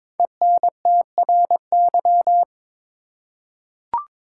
Morse code, heard as a keyed tone, about 22 wpm, 700 Hz. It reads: ENTRY
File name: 5 Letter Words - First 3 Letters Given - 22wpm